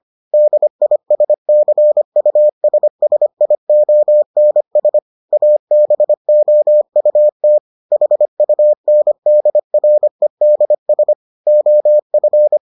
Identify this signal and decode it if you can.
Morse code, heard as a keyed tone, about 25 wpm, 600 Hz. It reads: DISCUSSIONS ABOUT HUNDREDS OF